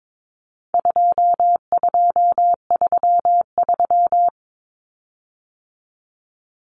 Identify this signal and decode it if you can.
Morse code, heard as a keyed tone, about 22 wpm, 700 Hz. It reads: 2233
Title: Common Cut Numbers - 4-Digit 22wpm